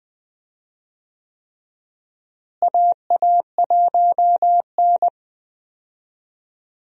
Morse code, heard as a keyed tone, about 21 wpm, 700 Hz. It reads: AA1N
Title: USA Call Signs - 4 Characters - Repeated 3x - 20wpm